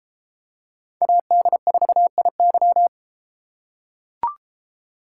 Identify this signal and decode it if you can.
Morse code, heard as a keyed tone, about 32 wpm, 700 Hz. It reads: AD4IY